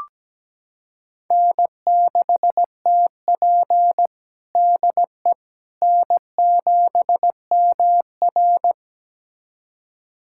Morse code, heard as a keyed tone, about 17 wpm, 700 Hz. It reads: N6TP DE N7MR
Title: USA Callsign DE Callsign - 4 Characters - 17wpm